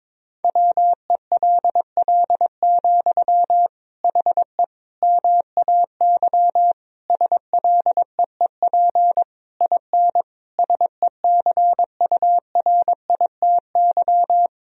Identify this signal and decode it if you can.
Morse code, heard as a keyed tone, about 21 wpm, 700 Hz. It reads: WELL, HE MAY SLEEP IN SECURITY
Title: Sentences from Top 1000 Words - Review Part 4 - 22wpm